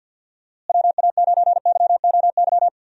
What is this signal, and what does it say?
Morse code, heard as a keyed tone, about 50 wpm, 700 Hz. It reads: WA9YOY
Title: USA Call Signs - 6 Characters - Repeated 2x - 50wpm